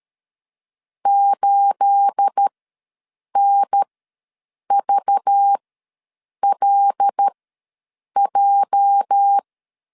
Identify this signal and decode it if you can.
Morse code, heard as a keyed tone, about 13 wpm, spaced as for 7 wpm, 800 Hz. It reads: 8NVLJ
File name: LEZIONI / 9 28 ws 12wpm 3354